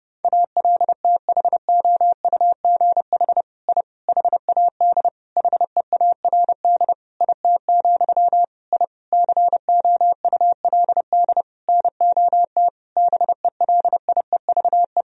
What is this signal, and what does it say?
Morse code, heard as a keyed tone, about 30 wpm, 700 Hz. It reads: ALTHOUGH I HAD HEARD IT, I COULD NOT BELIEVE
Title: Sentences from Top 1000 Words - Review Part 3 - 30wpm